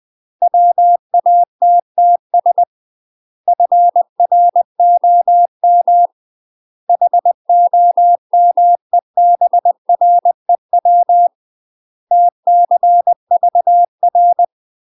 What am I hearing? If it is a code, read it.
Morse code, heard as a keyed tone, about 20 wpm, 700 Hz. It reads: WATTS FROM HOMEBREW TCVR
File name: UK MASTER QSOs TRAINING / 20WPM / UK MASTER QSOs 0032